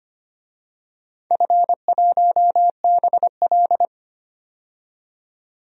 Morse code, heard as a keyed tone, about 25 wpm, 700 Hz. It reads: F1BL
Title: DXCC-100 Call Signs - 4 Characters - 25wpm